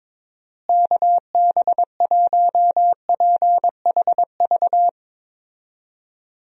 Morse code, heard as a keyed tone, about 22 wpm, 700 Hz. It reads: KB1PHV
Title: USA Call Signs - 6 Characters - Repeated 2x - 22wpm